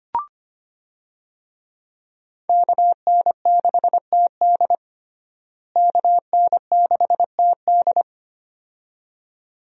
Morse code, heard as a keyed tone, about 25 wpm, 700 Hz. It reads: KN6TD KN6TD
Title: USA Call Signs - 5 Characters - Repeated 2x - 25wpm